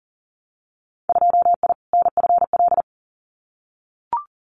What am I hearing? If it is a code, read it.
Morse code, heard as a keyed tone, about 40 wpm, 700 Hz. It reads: 2I NFL
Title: Winter Field Day Exchanges 40wpm